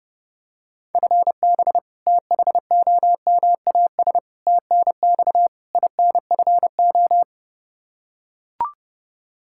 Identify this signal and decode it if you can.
Morse code, heard as a keyed tone, about 30 wpm, 700 Hz. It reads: FB THOMAS TNX INFO
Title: Practice QSO - Exchange 3 - More Info - 30wpm